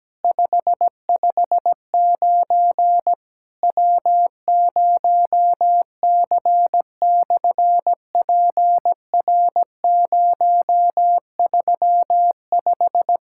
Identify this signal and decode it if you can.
Morse code, heard as a keyed tone, about 17 wpm, 700 Hz. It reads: 559 W0C/PR035
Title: SOTA QSO Practice 17wpm